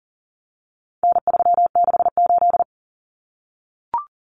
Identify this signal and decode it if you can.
Morse code, heard as a keyed tone, about 40 wpm, 700 Hz. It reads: N368
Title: Common Cut Numbers - 4-Digit 40wpm